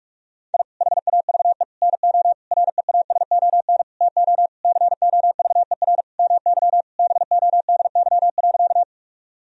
Morse code, heard as a keyed tone, about 46 wpm, 700 Hz. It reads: I HAVE NO REASON TO COVER MY BODY.
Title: Sentences from Top 1000 Words - Review Part 1 - 45wpm